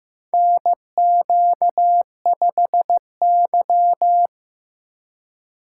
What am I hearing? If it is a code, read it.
Morse code, heard as a keyed tone, about 16 wpm, 700 Hz. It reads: NQ5Y
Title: USA Call Signs - 4 Characters - 3x Character Spacing - 15wpm